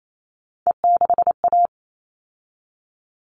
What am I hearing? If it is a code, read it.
Morse code, heard as a keyed tone, about 28 wpm, 700 Hz. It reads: E6A
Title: All Cut Numbers - 3-Digit 28wpm